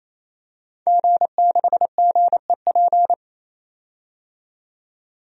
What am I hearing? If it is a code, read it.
Morse code, heard as a keyed tone, about 28 wpm, 700 Hz. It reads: G6GEP
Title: DXCC-100 Call Signs - 5 Characters - 28wpm